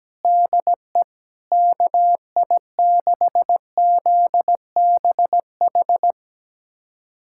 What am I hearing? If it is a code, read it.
Morse code, heard as a keyed tone, about 17 wpm, 700 Hz. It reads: DE KI6ZBH
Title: USA Callsign DE Callsign 17wpm